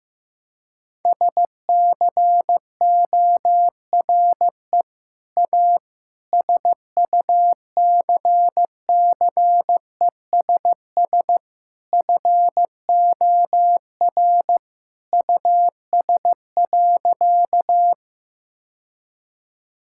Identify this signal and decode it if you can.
Morse code, heard as a keyed tone, about 16 wpm, 700 Hz. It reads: SCORE A SUCCESS FOR US.
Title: Sentences from Top 1600 Words 15wpm